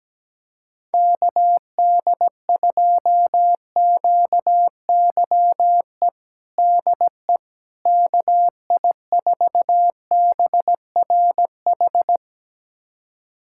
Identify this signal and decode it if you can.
Morse code, heard as a keyed tone, about 17 wpm, 700 Hz. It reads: KD2QYE DE KI4BRH
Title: USA Callsign DE Callsign 17wpm